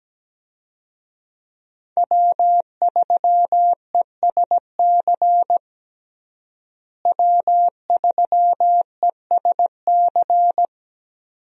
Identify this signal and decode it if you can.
Morse code, heard as a keyed tone, about 17 wpm, 700 Hz. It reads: W3ESC W3ESC
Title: USA Call Signs - 5 Characters - Repeated 2x - 17wpm